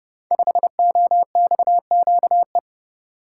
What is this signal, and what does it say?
Morse code, heard as a keyed tone, about 30 wpm, 700 Hz. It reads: 5OXQE
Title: Code Groups - 5 Letter:Numbers - 30wpm